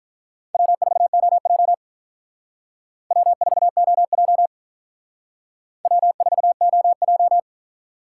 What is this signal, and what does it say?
Morse code, heard as a keyed tone, about 45 wpm, 700 Hz. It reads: W4OJ W4OJ W4OJ